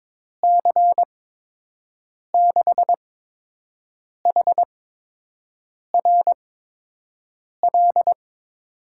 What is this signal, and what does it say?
Morse code, heard as a keyed tone, about 22 wpm, 700 Hz. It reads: C 6 H R L
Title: USA Call Signs - 6 Characters - 8x Character Spacing - 22wpm